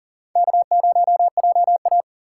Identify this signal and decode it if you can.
Morse code, heard as a keyed tone, about 40 wpm, 700 Hz. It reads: K0JA
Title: USA Call Signs - 4 Characters - 8x Character Spacing - 40wpm